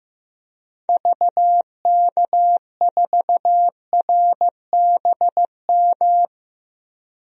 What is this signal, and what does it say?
Morse code, heard as a keyed tone, about 15 wpm, 700 Hz. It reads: VK4RBM
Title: DXCC-100 Call Signs - 6 Characters - 15wpm